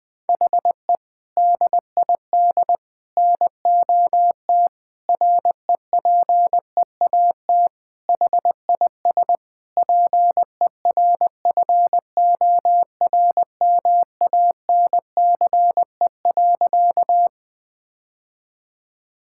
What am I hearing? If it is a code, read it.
Morse code, heard as a keyed tone, about 20 wpm, 700 Hz. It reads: HE DID NOT REPEAT HIS PERFORMANCE.
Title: Sentences from Top 1600 Words 20wpm